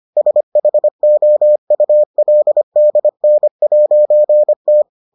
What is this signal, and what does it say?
Morse code, heard as a keyed tone, about 25 wpm, 600 Hz. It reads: SHOULDN'T